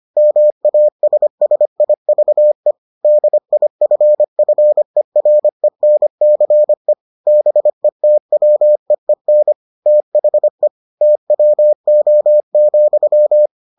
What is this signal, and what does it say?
Morse code, heard as a keyed tone, about 25 wpm, 600 Hz. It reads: MASSIVE DIFFERENCE BETWEEN THE TWO,